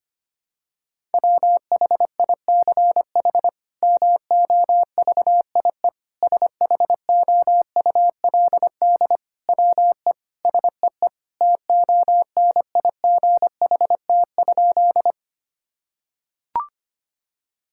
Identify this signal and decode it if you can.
Morse code, heard as a keyed tone, about 25 wpm, 700 Hz. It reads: WHICH MOVIE SHOULD WE SEE TONIGHT?
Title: Sentences from Top 1000 Words - Review Part 5 - 25wpm